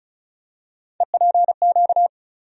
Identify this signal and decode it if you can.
Morse code, heard as a keyed tone, about 34 wpm, 700 Hz. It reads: EPQ
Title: USA Call Sign Postfixes - 3 Letters 35wpm